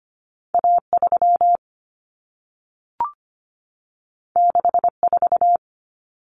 Morse code, heard as a keyed tone, about 25 wpm, 700 Hz. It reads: A3 64
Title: Common Cut Numbers - 2-Digit 25wpm